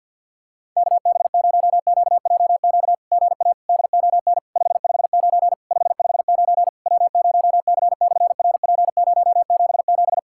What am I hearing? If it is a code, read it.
Morse code, heard as a keyed tone, about 50 wpm, 700 Hz. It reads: KB0QJQ GA DON 559 559 W0C/RP077
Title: SOTA QSO Practice 50wpm